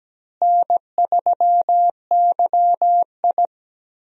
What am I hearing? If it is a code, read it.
Morse code, heard as a keyed tone, about 17 wpm, 700 Hz. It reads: N3YI
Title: USA Call Signs - 4 Characters - Repeated 2x - 17wpm